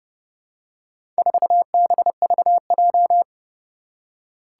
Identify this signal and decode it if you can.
Morse code, heard as a keyed tone, about 30 wpm, 700 Hz. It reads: 4BVJ
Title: Four Challenging Letter-Number Combinations 30wpm